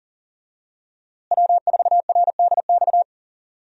Morse code, heard as a keyed tone, about 40 wpm, 700 Hz. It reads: W4RDX
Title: USA Call Signs - 5 Characters - Repeated 2x - 40wpm